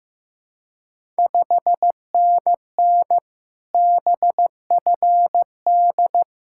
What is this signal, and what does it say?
Morse code, heard as a keyed tone, about 15 wpm, 700 Hz. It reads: 5NN BFD
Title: Ontario QSO Party Exchange 15wpm